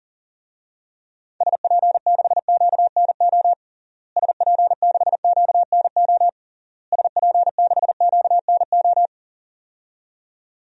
Morse code, heard as a keyed tone, about 40 wpm, 700 Hz. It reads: SP6QNO SP6QNO SP6QNO
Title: DXCC-100 Call Signs - 6 Characters - Repeated 3x - 40wpm